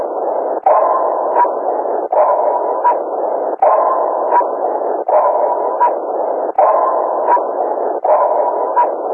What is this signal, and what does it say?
There is no code.